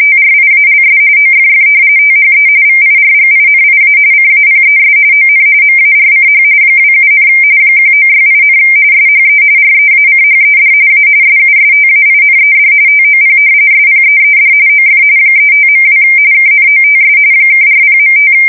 RTTY, about 45.45 baud. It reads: TRIALS OFF THE VIRGINIA CAPES ON MARCH 27, 1968, SHE REACHED A SUSTAINED TOP SPEED OF 35.2 KNOTS (40.5
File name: fsk170 / battleshipnj